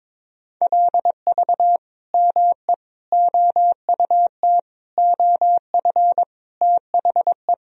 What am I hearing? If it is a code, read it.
Morse code, heard as a keyed tone, about 22 wpm, 700 Hz. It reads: LV ME OUT OF THE
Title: Sentences from Top 400 Words and Abbreviations 22wpm